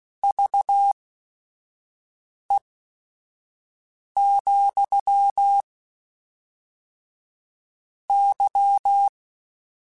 Morse code, heard as a keyed tone, about 16 wpm, spaced as for 5 wpm, 800 Hz. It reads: VE, Y